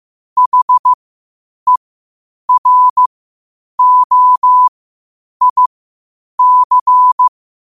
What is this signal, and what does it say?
Morse code, heard as a keyed tone, about 15 wpm, spaced as for 8 wpm, 1000 Hz. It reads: HEROIC
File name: qotdfiles / 8wpm